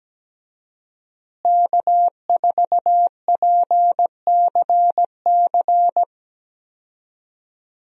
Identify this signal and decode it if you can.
Morse code, heard as a keyed tone, about 17 wpm, 700 Hz. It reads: K4PCC
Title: USA Call Signs - 5 Characters - Repeated 3x - 17wpm